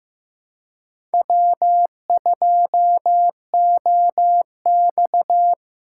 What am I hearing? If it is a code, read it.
Morse code, heard as a keyed tone, about 15 wpm, 700 Hz. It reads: W2OX
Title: USA Call Signs - 4 Characters - 2x Character Spacing - 15wpm